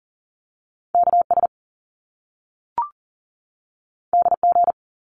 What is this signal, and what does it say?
Morse code, heard as a keyed tone, about 42 wpm, 700 Hz. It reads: KS DG
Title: Top Two Letter Combinations in Words 40wpm